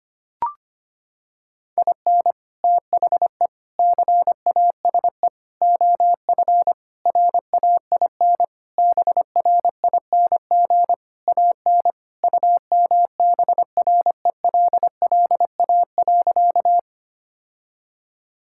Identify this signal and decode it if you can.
Morse code, heard as a keyed tone, about 25 wpm, 700 Hz. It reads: IN THE CASE OF RAIN BRING AN UMBRELLA.
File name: practice / 25wpm / Sentences with 3-Word Phrases - 25wpm